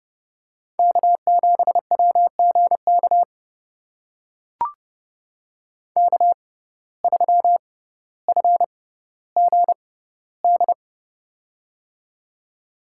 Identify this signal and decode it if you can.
Morse code, heard as a keyed tone, about 32 wpm, spaced as for 16 wpm, 700 Hz. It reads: K7WGK K3FGD
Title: USA Call Signs - 5 Characters - 6x Character Spacing - 30wpm